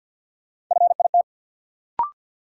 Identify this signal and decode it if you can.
Morse code, heard as a keyed tone, about 47 wpm, 700 Hz. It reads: FIT
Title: Sentences from Top 800 Words - New Word Drill - 50wpm